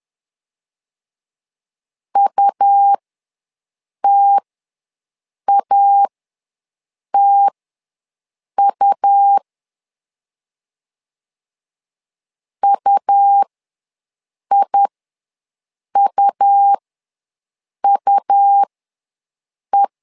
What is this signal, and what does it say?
Morse code, heard as a keyed tone, about 11 wpm, spaced as for 6 wpm, 800 Hz. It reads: UTATU UIUUE
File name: LEZIONI / 2 UHDN / 2 U ws 10wpm 2017